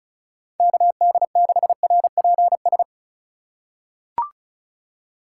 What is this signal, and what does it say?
Morse code, heard as a keyed tone, about 34 wpm, 700 Hz. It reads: KD6RPS
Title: USA Call Signs - 6 Characters - 3x Character Spacing - 35wpm